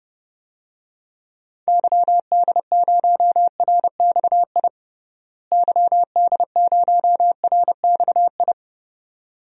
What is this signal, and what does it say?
Morse code, heard as a keyed tone, about 30 wpm, 700 Hz. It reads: YD0RXI YD0RXI
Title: DXCC-100 Call Signs - 6 Characters - Repeated 2x - 30wpm